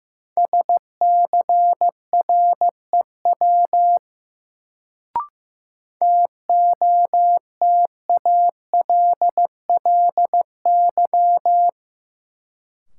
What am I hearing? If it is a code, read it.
Morse code, heard as a keyed tone, about 15 wpm, 700 Hz. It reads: SCREW TOTALLY
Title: Sentences from Top 1600 Words - New Word Drill - 15wpm